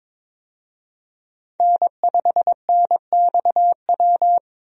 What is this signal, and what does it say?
Morse code, heard as a keyed tone, about 21 wpm, 700 Hz. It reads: N5NXW
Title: USA Call Signs - 5 Characters - Repeated 2x - 22wpm